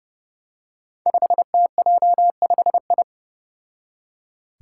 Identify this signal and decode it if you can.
Morse code, heard as a keyed tone, about 32 wpm, 700 Hz. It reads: 5TJ5I
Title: Five Challenging Letter-Number Combinations 30wpm